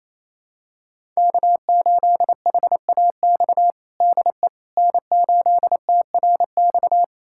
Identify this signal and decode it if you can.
Morse code, heard as a keyed tone, about 28 wpm, 700 Hz. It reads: K8HAX DE N8TRX
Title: USA Callsign DE Callsign - 5 Characters - 28wpm